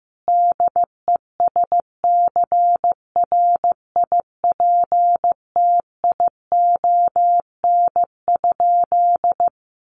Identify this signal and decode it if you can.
Morse code, heard as a keyed tone, about 15 wpm, 700 Hz. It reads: DESCRIPTION?
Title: Sentences from Top 1500 Words 15wpm